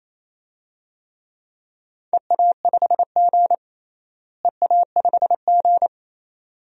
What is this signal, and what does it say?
Morse code, heard as a keyed tone, about 28 wpm, 700 Hz. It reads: EA5G EA5G
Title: DXCC-100 Call Signs - 4 Characters - Repeated 2x - 28wpm